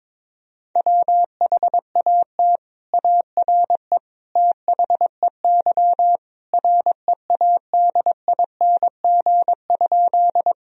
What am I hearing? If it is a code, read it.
Morse code, heard as a keyed tone, about 21 wpm, 700 Hz. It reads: WHAT ARE THEY READING?
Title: Sentences from Top 1000 Words - Review Part 3 - 22wpm